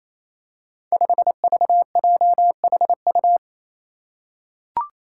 Morse code, heard as a keyed tone, about 27 wpm, 700 Hz. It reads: 5VJHU
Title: Five Challenging Letter-Number Combinations 28wpm